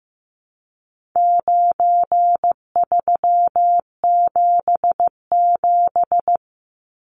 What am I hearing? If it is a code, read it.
Morse code, heard as a keyed tone, about 15 wpm, 700 Hz. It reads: 9377
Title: Numbers - 4-Digit 15wpm